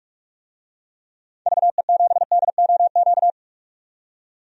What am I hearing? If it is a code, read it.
Morse code, heard as a keyed tone, about 46 wpm, 700 Hz. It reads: VE7DOQ
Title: DXCC-100 Call Signs - 6 Characters - 45wpm